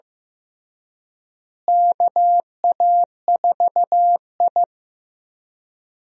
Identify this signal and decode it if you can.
Morse code, heard as a keyed tone, about 16 wpm, 700 Hz. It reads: KA4I
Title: USA Call Signs - 4 Characters - Repeated 3x - 15wpm